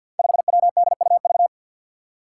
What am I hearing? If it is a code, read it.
Morse code, heard as a keyed tone, about 48 wpm, 700 Hz. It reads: 5WDUV
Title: Five Challenging Letter-Number Combinations 50wpm